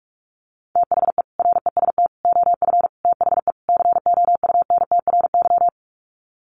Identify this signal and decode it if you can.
Morse code, heard as a keyed tone, about 44 wpm, 700 Hz. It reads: THE REST OF THE COUNTRY
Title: Sets of 5 Words 45wpm